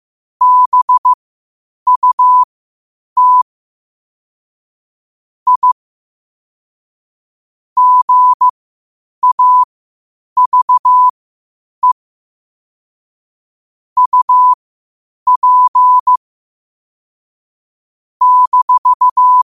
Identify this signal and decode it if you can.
Morse code, heard as a keyed tone, about 15 wpm, spaced as for 8 wpm, 1000 Hz. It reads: BUT I GAVE UP -
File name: qotdfiles / 8wpm